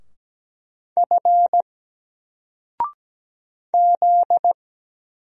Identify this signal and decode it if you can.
Morse code, heard as a keyed tone, about 17 wpm, 700 Hz. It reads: F Z
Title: Single Character (Letter:Number:Punctuation:Prosign) 17wpm